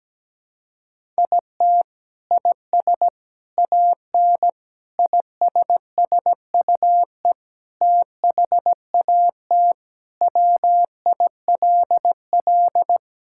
Morse code, heard as a keyed tone, about 17 wpm, 700 Hz. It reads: IT IS AN ISSUE THAT WILL
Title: Sentences from Top 1000 Words - Review Part 6 - 17wpm